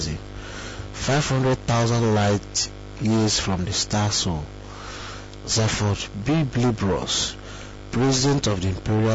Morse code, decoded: none